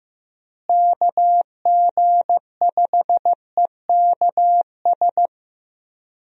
Morse code, heard as a keyed tone, about 16 wpm, 700 Hz. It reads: KG5EKS
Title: USA Call Signs - 6 Characters - Repeated 2x - 15wpm